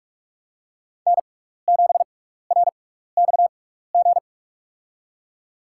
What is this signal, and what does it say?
Morse code, heard as a keyed tone, about 44 wpm, spaced as for 15 wpm, 700 Hz. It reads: N7RXG
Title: USA Call Signs - 5 Characters - 6x Character Spacing - 45wpm